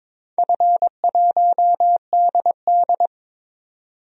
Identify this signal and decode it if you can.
Morse code, heard as a keyed tone, about 22 wpm, 700 Hz. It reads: F1DD
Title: DXCC-100 Call Signs - 4 Characters - 22wpm